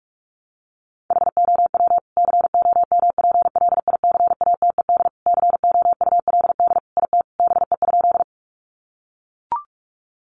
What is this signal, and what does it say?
Morse code, heard as a keyed tone, about 44 wpm, 700 Hz. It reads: HOW COMPLICATED COULD IT BE?
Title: Sentences from Top 1200 Words 45wpm